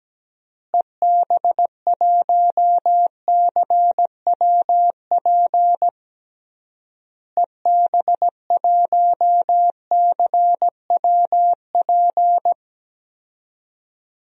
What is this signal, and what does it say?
Morse code, heard as a keyed tone, about 17 wpm, 700 Hz. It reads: EB1CWP EB1CWP
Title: DXCC-100 Call Signs - 6 Characters - Repeated 3x - 17wpm